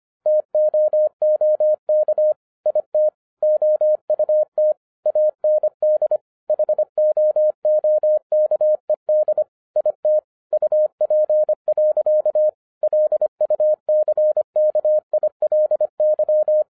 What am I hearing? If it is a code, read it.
Morse code, heard as a keyed tone, about 25 wpm, 610 Hz. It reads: TOOK IT OUT AND HOOKED IT UP. LUCKILY
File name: FIRST FISTS CONTACT-25wpm